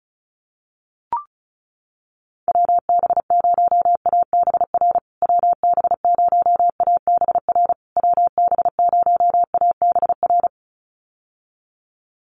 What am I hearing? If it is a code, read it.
Morse code, heard as a keyed tone, about 34 wpm, 700 Hz. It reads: WB0ABR WB0ABR WB0ABR
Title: USA Call Signs Repeated 3x 35wpm